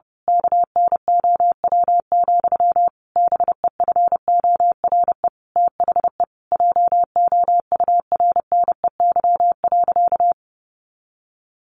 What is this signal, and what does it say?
Morse code, heard as a keyed tone, about 30 wpm, 700 Hz. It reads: KNOW, BEFORE THE JOURNEY.
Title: A Sound Of Thunder 30wpm